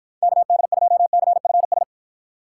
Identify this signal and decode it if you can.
Morse code, heard as a keyed tone, about 53 wpm, 700 Hz. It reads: KD2CFS